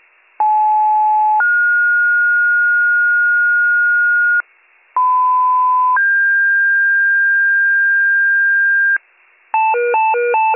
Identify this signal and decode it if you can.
Two-tone page, heard as a one-second tone, then a three-second tone, about 860 then 1430 Hz, about 1010 then 1600 Hz, with noise. It is heard from 0.4 to 9.0 s.